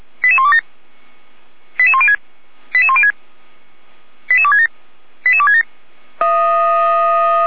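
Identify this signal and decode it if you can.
Five-tone selective call (ZVEI), heard as quick runs of five tones, tones of about 70 ms. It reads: 88127 88106 88106 88267 88267
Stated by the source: ZVEI five-tone call received by SDR.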